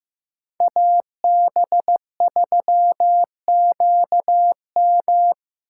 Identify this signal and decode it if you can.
Morse code, heard as a keyed tone, about 15 wpm, 700 Hz. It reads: AB3QM
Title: USA Call Signs - 5 Characters - 3x Character Spacing - 15wpm